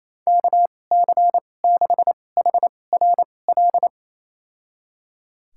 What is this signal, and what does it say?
Morse code, heard as a keyed tone, about 27 wpm, spaced as for 20 wpm, 700 Hz. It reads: KC6HRL
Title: USA Call Signs - 6 Characters - 2x Character Spacing - 28wpm